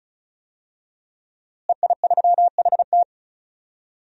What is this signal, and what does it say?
Morse code, heard as a keyed tone, about 34 wpm, 700 Hz. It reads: EI3HT
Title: DXCC-100 Call Signs - 5 Characters - Repeated 3x - 35wpm